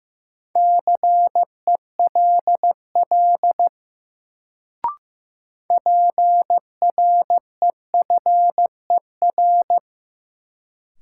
Morse code, heard as a keyed tone, about 15 wpm, 700 Hz. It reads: CELL PREFER
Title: Sentences from Top 1100 Words - New Word Drill - 15wpm